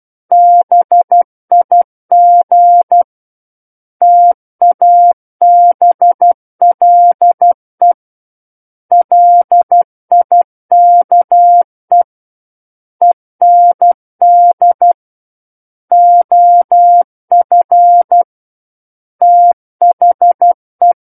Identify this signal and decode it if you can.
Morse code, heard as a keyed tone, about 12 wpm, 700 Hz. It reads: BIG TABLE LIKE END OF THE
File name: Book One / 03-Chapter 3 - 12 WPM